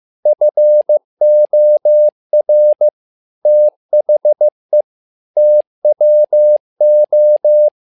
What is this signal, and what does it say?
Morse code, heard as a keyed tone, about 15 wpm, 600 Hz. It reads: FOR THE TWO
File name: learning center / tower trouble / tower trouble 15wpm